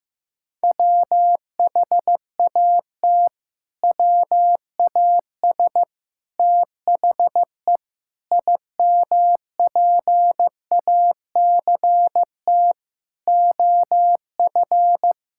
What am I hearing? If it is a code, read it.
Morse code, heard as a keyed tone, about 16 wpm, 700 Hz. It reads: WHAT WAS THE IMPACT OF
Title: Sentences from Top 1000 Words - Review Part 6 - 15wpm